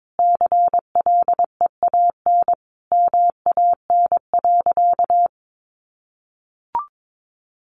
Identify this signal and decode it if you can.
Morse code, heard as a keyed tone, about 22 wpm, 700 Hz. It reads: CLEAN MAN.
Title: Sentences from Top 1400 Words 22wpm